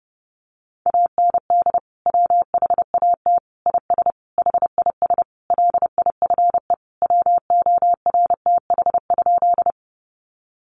Morse code, heard as a keyed tone, about 28 wpm, 700 Hz. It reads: AND WHAT IS HIS LIFE WORTH?
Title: Sentences from Top 800 Words 30wpm